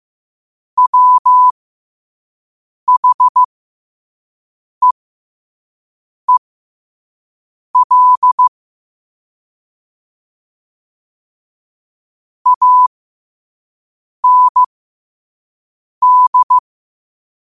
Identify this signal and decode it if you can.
Morse code, heard as a keyed tone, about 15 wpm, spaced as for 5 wpm, 1000 Hz. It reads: WHEEL AND
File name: 5wpm